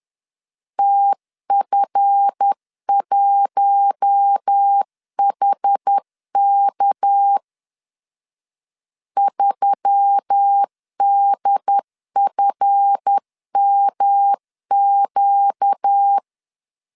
Morse code, heard as a keyed tone, about 10 wpm, 800 Hz. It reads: TF1HK 3DFMQ
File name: LEZIONI / 7 GF9Z / 7 GF 10wpm 2514